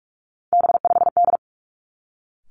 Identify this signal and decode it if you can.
Morse code, heard as a keyed tone, about 46 wpm, 700 Hz. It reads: B5D
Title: Three Challenging Letter-Number Combinations 45wpm